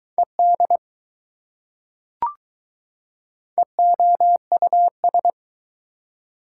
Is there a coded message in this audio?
Yes: Morse code from 0.2 to 5.3 s, heard as a keyed tone, about 20 wpm, 700 Hz. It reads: ED EOUS